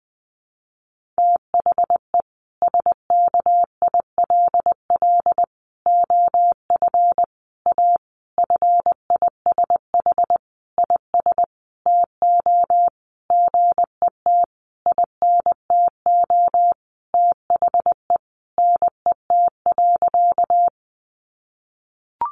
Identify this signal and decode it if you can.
Morse code, heard as a keyed tone, about 20 wpm, 700 Hz. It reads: THE SKILL OF A FISH IS TO GET INTO THE NET.
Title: Sentences from Top 1400 Words 20wpm